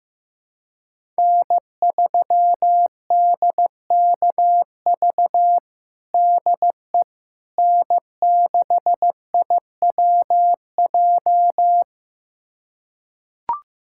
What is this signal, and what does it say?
Morse code, heard as a keyed tone, about 15 wpm, 700 Hz. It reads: N3DKV DE N6IWJ
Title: USA Callsign DE Callsign - 5 Characters - 15wpm